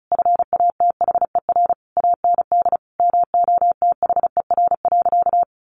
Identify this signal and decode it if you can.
Morse code, heard as a keyed tone, about 34 wpm, 700 Hz. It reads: FATHER AND MOTHER.
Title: Sentences from Top 1100 Words 35wpm